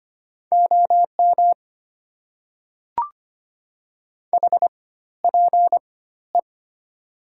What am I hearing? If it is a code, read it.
Morse code, heard as a keyed tone, about 25 wpm, spaced as for 14 wpm, 690 Hz. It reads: OM HPE
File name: Top 90 QSO Elements - 4x Character Spacing - 25wpm